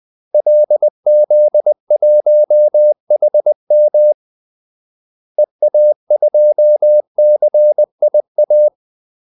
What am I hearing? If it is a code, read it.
Morse code, heard as a keyed tone, about 20 wpm, 600 Hz. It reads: LZ1HM EA2CIA
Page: Kaldesignaler 20 wpm